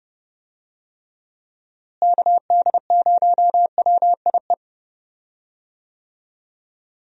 Morse code, heard as a keyed tone, about 30 wpm, 700 Hz. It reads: KD0WIE
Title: USA Call Signs - 6 Characters - Repeated 2x - 30wpm